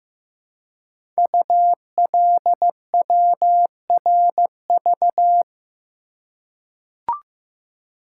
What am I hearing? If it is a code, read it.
Morse code, heard as a keyed tone, about 15 wpm, 700 Hz. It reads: ULWRV
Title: Code Groups - 5 Letters - 15wpm